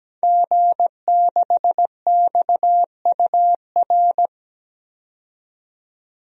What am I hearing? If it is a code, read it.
Morse code, heard as a keyed tone, about 17 wpm, 700 Hz. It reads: G6XUR
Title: DXCC-100 Call Signs - 5 Characters - 17wpm